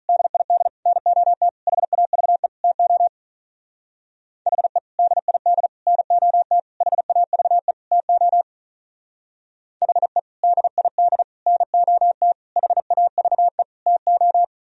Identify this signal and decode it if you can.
Morse code, heard as a keyed tone, about 42 wpm, 700 Hz. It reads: DID NOT HAVE TO HE DID NOT HAVE TO HE DID NOT HAVE TO
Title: Speed-Racing - Sets of 5 Words 35wpm